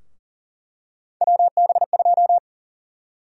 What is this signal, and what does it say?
Morse code, heard as a keyed tone, about 38 wpm, 700 Hz. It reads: WB2
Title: USA Call Sign Prefixes - 40wpm